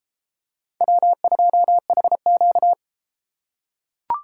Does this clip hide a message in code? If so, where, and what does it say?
Yes: Morse code from 0.8 to 2.7 s, heard as a keyed tone, about 31 wpm, 700 Hz. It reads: W2HQ